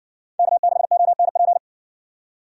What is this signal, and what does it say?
Morse code, heard as a keyed tone, about 55 wpm, 700 Hz. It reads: K6ONP